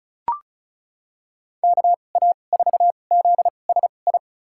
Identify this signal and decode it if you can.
Morse code, heard as a keyed tone, about 36 wpm, spaced as for 25 wpm, 700 Hz. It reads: KA4ZSI
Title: USA Call Signs - 2x Character Spacing - 35wpm